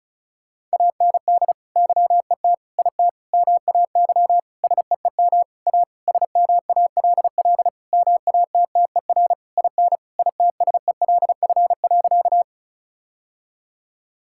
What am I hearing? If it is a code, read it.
Morse code, heard as a keyed tone, about 36 wpm, 700 Hz. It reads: AND YET IT MAY SEEM A SMALL MATTER IN ITSELF.
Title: Sentences from Top 1000 Words - Review Part 5 - 35wpm